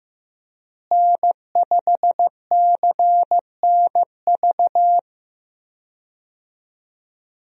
Morse code, heard as a keyed tone, about 15 wpm, 700 Hz. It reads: N5CNV
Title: USA Call Signs - 5 Characters - Repeated 3x - 15wpm